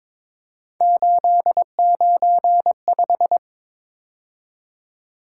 Morse code, heard as a keyed tone, about 26 wpm, 700 Hz. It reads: 895
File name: Speed-Racing - Three Letter-Number Combinations - 22wpm